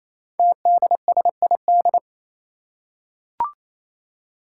Morse code, heard as a keyed tone, about 27 wpm, 700 Hz. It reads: TDSID
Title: Five Challenging Letter-Number Combinations 28wpm